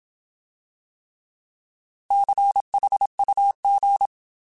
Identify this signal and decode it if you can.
Morse code, heard as a keyed tone, about 27 wpm, 800 Hz. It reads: CHUG
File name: Alphabetical